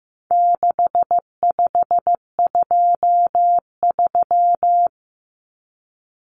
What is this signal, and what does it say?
Morse code, heard as a keyed tone, about 15 wpm, 700 Hz. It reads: 6523
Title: Numbers - 4-Digit 15wpm